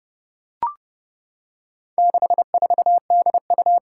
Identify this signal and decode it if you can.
Morse code, heard as a keyed tone, about 30 wpm, 700 Hz. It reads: 64DU
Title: Four Challenging Letter-Number Combinations 30wpm